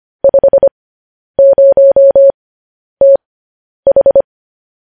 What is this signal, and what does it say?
Morse code, heard as a keyed tone, about 25 wpm, spaced as for 10 wpm, 550 Hz. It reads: 50TH
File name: podcast / News headlines-1x-25x10